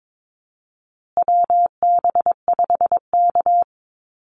Morse code, heard as a keyed tone, about 22 wpm, 700 Hz. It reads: WB5K
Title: Call Signs Repeated 2x - 22wpm